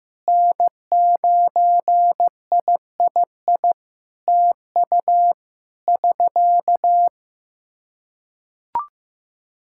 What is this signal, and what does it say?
Morse code, heard as a keyed tone, about 15 wpm, 700 Hz. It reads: N9III TU <SK>
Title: K1USN SST QSO Practice 15wpm